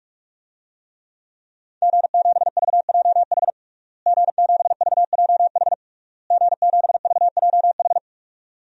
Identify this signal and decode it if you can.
Morse code, heard as a keyed tone, about 46 wpm, 700 Hz. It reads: G7VJH G7VJH G7VJH
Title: DXCC-100 Call Signs - 5 Characters - Repeated 3x - 45wpm